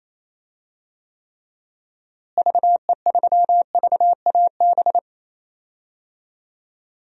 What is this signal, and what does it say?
Morse code, heard as a keyed tone, about 28 wpm, 700 Hz. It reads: VE3VAB
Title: DXCC-100 Call Signs - 6 Characters - Repeated 3x - 28wpm